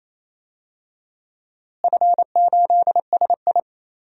Morse code, heard as a keyed tone, about 28 wpm, 700 Hz. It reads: F8SI
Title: DXCC-100 Call Signs - 4 Characters - Repeated 2x - 28wpm